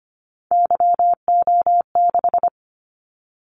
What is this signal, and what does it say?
Morse code, heard as a keyed tone, about 25 wpm, 700 Hz. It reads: YO6
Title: Callsign Prefixes for Top 25 DXCC - 25wpm